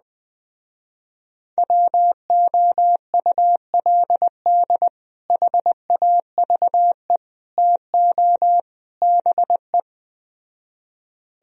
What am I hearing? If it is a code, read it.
Morse code, heard as a keyed tone, about 20 wpm, 700 Hz. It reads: WOULD HAVE TO BE
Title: Speed-Racing - Sets of 4 Words 17wpm